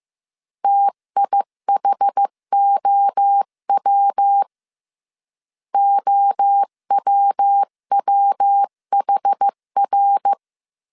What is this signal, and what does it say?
Morse code, heard as a keyed tone, about 15 wpm, 800 Hz. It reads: TIHOW OWWHR
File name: CW / DET / 1 3 1500C S A 14wpm 2447